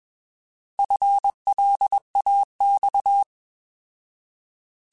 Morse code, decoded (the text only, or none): FLAX